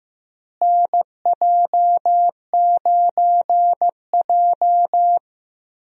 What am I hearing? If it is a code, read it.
Morse code, heard as a keyed tone, about 15 wpm, 700 Hz. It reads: NJ9J
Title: USA Call Signs - 4 Characters - Repeated 2x - 15wpm